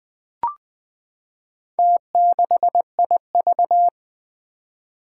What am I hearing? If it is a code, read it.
Morse code, heard as a keyed tone, about 20 wpm, 700 Hz. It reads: T6IV
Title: Four Challenging Letter-Number Combinations 20wpm